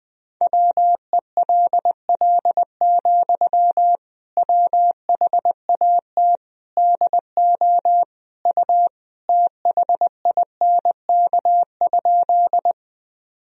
Morse code, heard as a keyed tone, about 21 wpm, 700 Hz. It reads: WELL, WHAT DO U THINK?
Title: Sentences from Top 200 Words and Abbreviations 20wpm